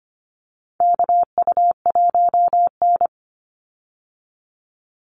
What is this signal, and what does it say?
Morse code, heard as a keyed tone, about 25 wpm, 700 Hz. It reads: KU1N
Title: Call Signs Repeated 3x - 25wpm